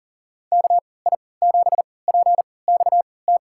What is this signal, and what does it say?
Morse code, heard as a keyed tone, about 40 wpm, spaced as for 23 wpm, 700 Hz. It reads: KI7PXT
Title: USA Call Signs - 6 Characters - 3x Character Spacing - 40wpm